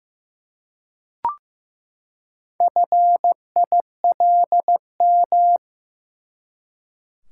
Morse code, heard as a keyed tone, about 15 wpm, 700 Hz. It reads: FILM
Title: Four Letter Words - 15wpm